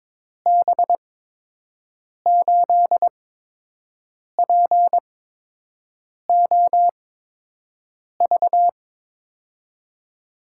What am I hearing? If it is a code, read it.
Morse code, heard as a keyed tone, about 22 wpm, spaced as for 6 wpm, 700 Hz. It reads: B8POV
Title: USA Call Signs - 6 Characters - 8x Character Spacing - 22wpm